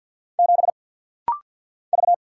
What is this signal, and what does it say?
Morse code, heard as a keyed tone, about 48 wpm, 690 Hz. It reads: 7 V
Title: Single Letter-Number - Rapid-Fire 50wpm